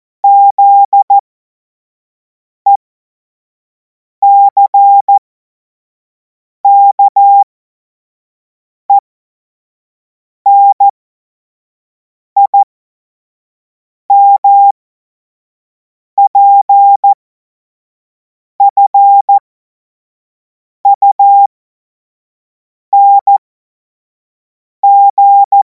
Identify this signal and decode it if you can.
Morse code, heard as a keyed tone, about 14 wpm, spaced as for 5 wpm, 800 Hz. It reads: ZECKENIMPFUNG
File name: Morsekurs / Witze / Witz 14. Juli 2026 25 F70